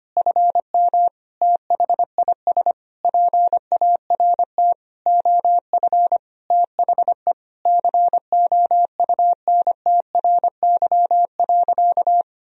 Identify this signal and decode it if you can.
Morse code, heard as a keyed tone, about 25 wpm, 700 Hz. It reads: FM THIS PART OF THE COUNTRY.
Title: Sentences from Top 300 Words and Abbreviations 25wpm